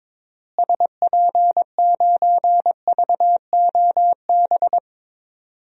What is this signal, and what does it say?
Morse code, heard as a keyed tone, about 22 wpm, 700 Hz. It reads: SP9VOB
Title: DXCC-100 Call Signs - 6 Characters - Repeated 2x - 22wpm